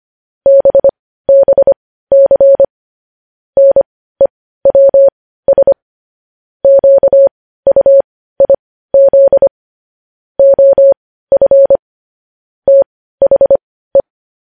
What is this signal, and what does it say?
Morse code, heard as a keyed tone, about 25 wpm, spaced as for 15 wpm, 550 Hz. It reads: BBC NEWS QUIZ OF THE
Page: News headlines at 15